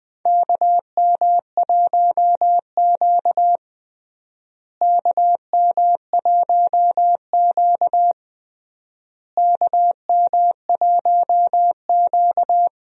Morse code, heard as a keyed tone, about 20 wpm, 700 Hz. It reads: KM1Q KM1Q KM1Q
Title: USA Call Signs - 4 Characters - Repeated 3x - 20wpm